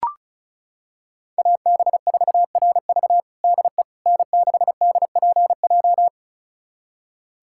Morse code, heard as a keyed tone, about 36 wpm, 700 Hz. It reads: AB4RV DE N6DPJ
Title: USA Callsign DE Callsign - 5 Characters - 35wpm